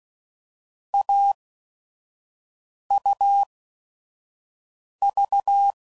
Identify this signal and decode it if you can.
Morse code, heard as a keyed tone, about 16 wpm, spaced as for 5 wpm, 800 Hz. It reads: AUV
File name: Combinations / Difficult Combos / AUV Short